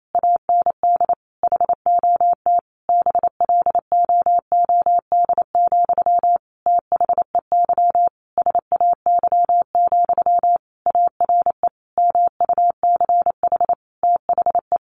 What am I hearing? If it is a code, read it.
Morse code, heard as a keyed tone, about 28 wpm, 700 Hz. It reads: AND HOT BLOOD, THEY SAY, ARE MUCH THE
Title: Sentences from Top 1100 Words 28wpm